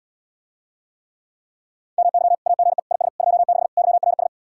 Morse code, heard as a keyed tone, about 77 wpm, 700 Hz. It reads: MY AGE IS 26 YRS